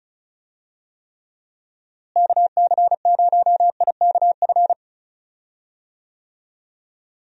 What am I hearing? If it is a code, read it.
Morse code, heard as a keyed tone, about 36 wpm, 700 Hz. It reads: KC0IKF
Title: USA Call Signs - 6 Characters - Repeated 3x - 35wpm